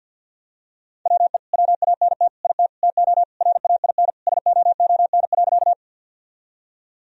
Morse code, heard as a keyed tone, about 48 wpm, 700 Hz. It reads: WE WANT IT TO RAIN SOON.
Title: Sentences from Top 1000 Words - Review Part 1 - 50wpm